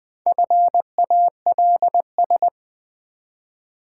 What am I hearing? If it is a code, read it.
Morse code, heard as a keyed tone, about 20 wpm, 700 Hz. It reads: FALS
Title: Build A Word - 1 Letter At A Time - 5-Letter Words - 20wpm